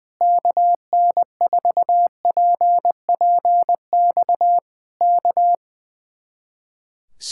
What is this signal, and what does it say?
Morse code, heard as a keyed tone, about 20 wpm, 700 Hz. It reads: KN4PPX K
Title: Speed-Racing Common QSO Phrases - 20wpm